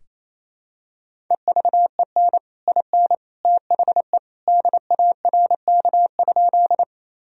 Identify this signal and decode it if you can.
Morse code, heard as a keyed tone, about 28 wpm, 700 Hz. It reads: EVEN IN THE DARK?
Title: Sentences from Top 1000 Words - Review Part 1 - 28wpm